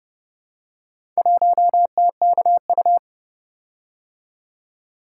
Morse code, heard as a keyed tone, about 30 wpm, 700 Hz. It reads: 1TKU